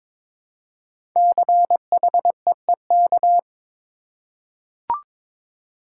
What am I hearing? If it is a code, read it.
Morse code, heard as a keyed tone, about 22 wpm, 700 Hz. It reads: CHEEK
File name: 5 Letter Words - First 3 Letters Given - 22wpm